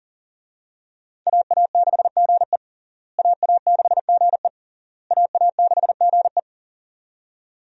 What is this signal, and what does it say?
Morse code, heard as a keyed tone, about 37 wpm, 700 Hz. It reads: AA6GE AA6GE AA6GE
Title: USA Call Signs - 5 Characters - Repeated 3x - 40wpm